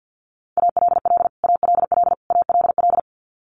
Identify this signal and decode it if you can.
Morse code, heard as a keyed tone, about 48 wpm, 700 Hz. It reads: ALL ALL ALL
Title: Top 200 Words Repeated 50wpm